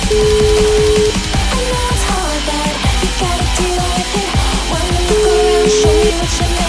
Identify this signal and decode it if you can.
Ringback tone (Europe, 425 Hz), with noise.